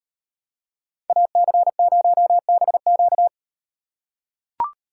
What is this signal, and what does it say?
Morse code, heard as a keyed tone, about 38 wpm, 700 Hz. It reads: AC0BQ